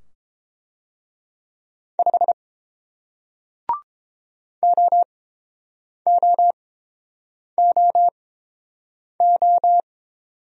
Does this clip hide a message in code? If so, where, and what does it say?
Yes: Morse code from 2.0 to 9.8 s, heard as a keyed tone, about 30 wpm, spaced as for 7 wpm, 700 Hz. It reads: 5 OOOO